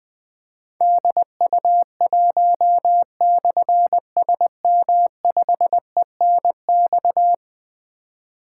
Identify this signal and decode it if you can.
Morse code, heard as a keyed tone, about 20 wpm, 700 Hz. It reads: DU1/SM5ENX
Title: Difficult Call Signs 20wpm